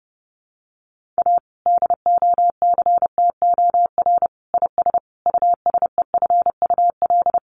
Morse code, heard as a keyed tone, about 30 wpm, 700 Hz. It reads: A DOCTOR IS USEFUL
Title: Sentences from Top 1500 Words 30wpm